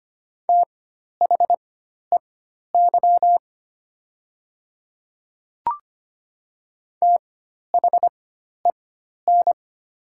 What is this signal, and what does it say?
Morse code, heard as a keyed tone, about 25 wpm, spaced as for 12 wpm, 700 Hz. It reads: THEY THEN
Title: 4x Character Spacing - Top 100 Words - 25wpm